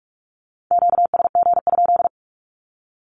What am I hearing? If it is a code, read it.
Morse code, heard as a keyed tone, about 44 wpm, 700 Hz. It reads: QSG?
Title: Q-Codes 45wpm